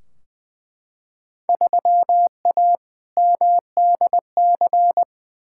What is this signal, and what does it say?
Morse code, heard as a keyed tone, about 20 wpm, 700 Hz. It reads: 3A MDC
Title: Field Day Exchanges - 20wpm